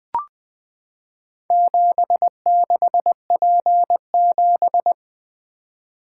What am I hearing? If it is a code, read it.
Morse code, heard as a keyed tone, about 20 wpm, 700 Hz. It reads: 76P7
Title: Four Letter-Number Combinations - 20wpm